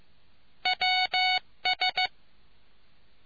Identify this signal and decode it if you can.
Morse code, heard as a keyed tone, about 15 wpm, 740 Hz. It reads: WS